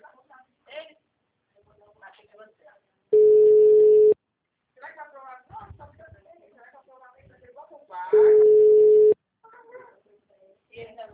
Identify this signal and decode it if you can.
Ringback tone (Europe, 425 Hz), with speech.
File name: out- -2015- - - .17568